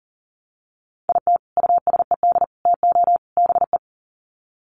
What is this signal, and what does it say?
Morse code, heard as a keyed tone, about 40 wpm, 700 Hz. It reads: IT USED TO BE
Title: Sets of 4 Words 40wpm